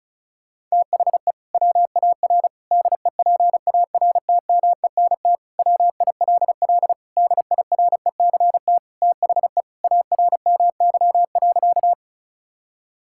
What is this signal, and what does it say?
Morse code, heard as a keyed tone, about 36 wpm, 700 Hz. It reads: THE WAR DEPARTMENT WILL DIRECT THE ARMY.
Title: Sentences from Top 1000 Words - Review Part 5 - 35wpm